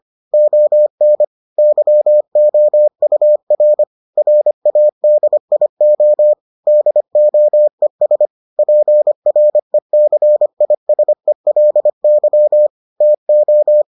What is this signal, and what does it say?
Morse code, heard as a keyed tone, about 25 wpm, 600 Hz. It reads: ON YOUR RADIO DOES PRECISELY TO